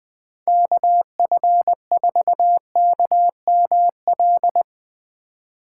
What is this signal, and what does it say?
Morse code, heard as a keyed tone, about 21 wpm, 700 Hz. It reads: KF4KML
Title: Speed Racing - USA 2x3 Call Signs 20wpm